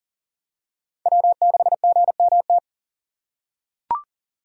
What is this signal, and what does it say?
Morse code, heard as a keyed tone, about 40 wpm, 700 Hz. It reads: W6GMT
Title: USA Call Signs - 2x Character Spacing - 40wpm